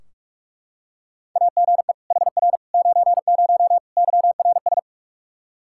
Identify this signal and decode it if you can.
Morse code, heard as a keyed tone, about 45 wpm, 700 Hz. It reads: AGE HR 90 YRS
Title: Practice QSO - Exchange 3 - More Info - 45wpm